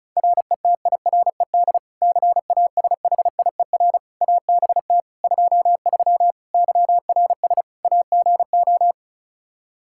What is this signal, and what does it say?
Morse code, heard as a keyed tone, about 34 wpm, 700 Hz. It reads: RETIRED CASHIER ABT 23 YRS AGO
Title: Practice QSO - Exchange 3 - More Info - 35wpm